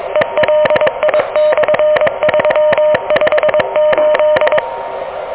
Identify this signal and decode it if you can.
Morse code, heard as a keyed tone, about 22 wpm, 600 Hz, with noise. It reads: ELI/358
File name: OK QRP 2023 / OK QRP 2023